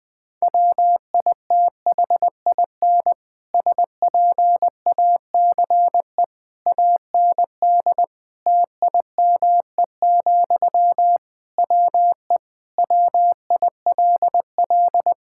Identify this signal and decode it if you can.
Morse code, heard as a keyed tone, about 21 wpm, 700 Hz. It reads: WITHIN SPACE AND TIME, WE WILL
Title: Sentences from Top 1000 Words - Review Part 5 - 20wpm